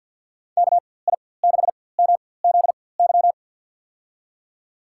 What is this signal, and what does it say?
Morse code, heard as a keyed tone, about 53 wpm, spaced as for 24 wpm, 700 Hz. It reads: KI6MZY
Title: USA Call Signs - 6 Characters - 4x Character Spacing - 50wpm